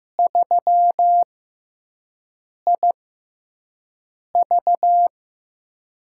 Morse code, heard as a keyed tone, about 15 wpm, spaced as for 5 wpm, 700 Hz. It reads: 3IV
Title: USA Call Signs - 5 Characters - 6x Character Spacing - 15wpm